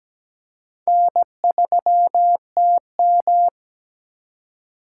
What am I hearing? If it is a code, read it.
Morse code, heard as a keyed tone, about 17 wpm, 700 Hz. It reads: N3TM
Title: USA Call Signs - 4 Characters - 4x Character Spacing - 17wpm